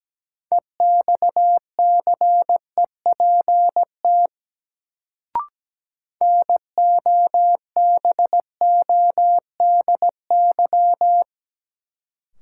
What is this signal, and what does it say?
Morse code, heard as a keyed tone, about 17 wpm, 700 Hz. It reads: EXCEPT NOBODY
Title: Sentences from Top 1600 Words - New Word Drill Alt - 17wpm